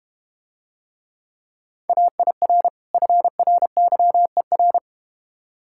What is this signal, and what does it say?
Morse code, heard as a keyed tone, about 32 wpm, 690 Hz. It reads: AIR FRYER